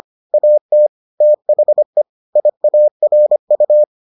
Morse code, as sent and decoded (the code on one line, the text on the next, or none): .- - / - .... / .. .- .-. ..-
AT THE IARU